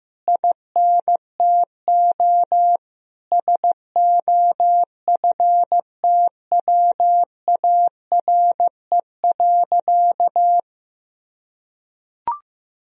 Morse code, heard as a keyed tone, about 14 wpm, 700 Hz. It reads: INTO SOFTWARE.
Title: Sentences from Top 1000 Words - Review Part 3 - 15wpm